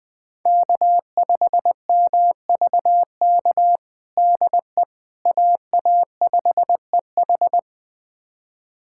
Morse code, heard as a keyed tone, about 20 wpm, 700 Hz. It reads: K5MVK DE AA5EH
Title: USA Callsign DE Callsign - 5 Characters - 20wpm